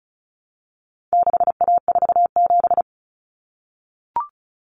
Morse code, heard as a keyed tone, about 34 wpm, 700 Hz. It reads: 6A47
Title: Common Cut Numbers - 4-Digit 35wpm